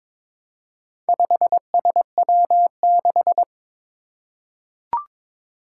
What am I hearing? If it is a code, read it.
Morse code, heard as a keyed tone, about 22 wpm, 700 Hz. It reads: 5SW6
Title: Four Challenging Letter-Number Combinations 22wpm